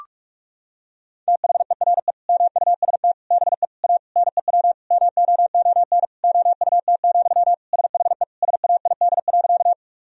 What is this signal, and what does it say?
Morse code, heard as a keyed tone, about 46 wpm, 700 Hz. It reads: THERE MUST BE A NEW MOON OUT, SHE SAID.
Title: Sentences from Top 1000 Words - Review Part 3 - 45wpm